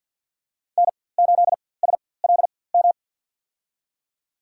Morse code, heard as a keyed tone, about 52 wpm, spaced as for 23 wpm, 700 Hz. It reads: N8SLM
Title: USA Call Signs - 5 Characters - 4x Character Spacing - 50wpm